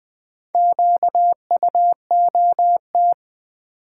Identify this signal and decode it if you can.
Morse code, heard as a keyed tone, about 20 wpm, 700 Hz. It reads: QUOT
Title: Build A Word - 1 Letter At A Time - 5-Letter Words - 20wpm